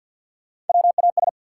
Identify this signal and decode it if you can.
Morse code, heard as a keyed tone, about 53 wpm, 700 Hz. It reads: WAS
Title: Sets of 2 Words - New Word Drill - 50wpm